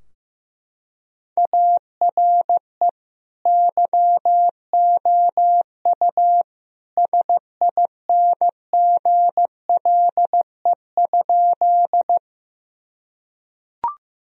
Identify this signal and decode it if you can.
Morse code, heard as a keyed tone, about 15 wpm, 700 Hz. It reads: ARE YOU SINGLE?
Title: Sentences from Top 1000 Words - Review Part 4 - 15wpm